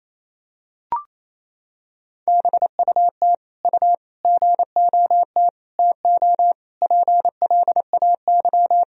Morse code, heard as a keyed tone, about 28 wpm, 700 Hz. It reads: BUT U GOT TO PLAY
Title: Sentences from Top 300 Words and Abbreviations 28wpm